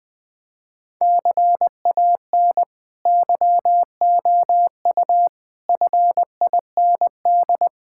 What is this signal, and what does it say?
Morse code, heard as a keyed tone, about 21 wpm, 700 Hz. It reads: CAN YOU FIND
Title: Sentences from Top 1600 Words 20wpm